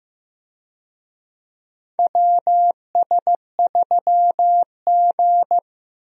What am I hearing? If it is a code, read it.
Morse code, heard as a keyed tone, about 15 wpm, 700 Hz. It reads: WS3G
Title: USA Call Signs - 4 Characters - Repeated 3x - 15wpm